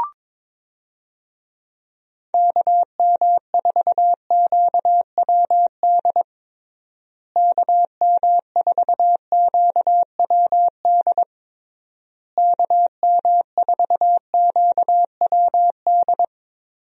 Morse code, heard as a keyed tone, about 22 wpm, 700 Hz. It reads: KM4QWD KM4QWD KM4QWD
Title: USA Call Signs - 6 Characters - Repeated 3x - 22wpm